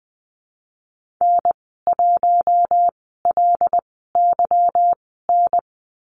Morse code, heard as a keyed tone, about 20 wpm, spaced as for 15 wpm, 700 Hz. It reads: N1LYN
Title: 147 Lesson 24g - Letter y - Call Signs 20wpm